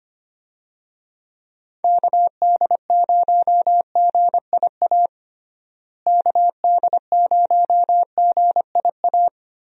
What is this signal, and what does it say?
Morse code, heard as a keyed tone, about 25 wpm, 700 Hz. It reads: KD0GIA KD0GIA
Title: USA Call Signs - 6 Characters - Repeated 3x - 25wpm